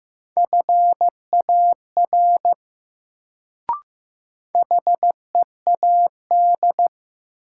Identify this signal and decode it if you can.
Morse code, heard as a keyed tone, about 15 wpm, 700 Hz. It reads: FAR HEAD
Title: Sentences from Top 300 Words - New Word Drill - 15wpm